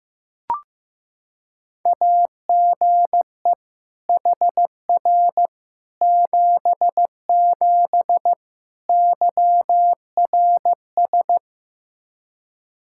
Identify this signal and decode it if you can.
Morse code, heard as a keyed tone, about 15 wpm, 700 Hz. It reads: AGE HR 77 YRS
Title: Practice QSO - Exchange 3 - More Info - 15wpm